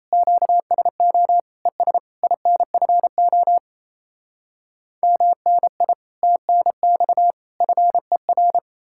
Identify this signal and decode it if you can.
Morse code, heard as a keyed tone, about 37 wpm, 700 Hz. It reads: QSO ES INFO MNI TNX FER